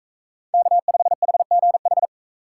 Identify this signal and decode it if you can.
Morse code, heard as a keyed tone, about 39 wpm, 700 Hz. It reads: K5HGH